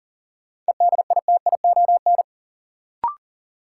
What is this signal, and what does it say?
Morse code, heard as a keyed tone, about 40 wpm, 700 Hz. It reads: EDITION
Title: Sets of 5 Words - New Word Drill - 40wpm